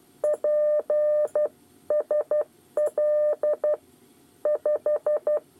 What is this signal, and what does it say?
Morse code, heard as a keyed tone, about 12 wpm, 600 Hz. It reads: P SL 5
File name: 8 bis lezione cw 2